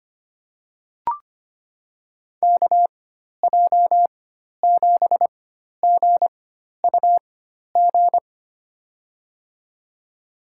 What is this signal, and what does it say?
Morse code, heard as a keyed tone, about 25 wpm, spaced as for 12 wpm, 700 Hz. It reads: KJ7GUG
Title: USA Call Signs - 6 Characters - 4x Character Spacing - 25wpm